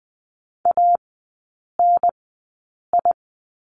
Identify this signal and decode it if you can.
Morse code, heard as a keyed tone, about 20 wpm, 700 Hz. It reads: A N I
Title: 029 Lesson 6e - Letter i - Three Characters 20wpm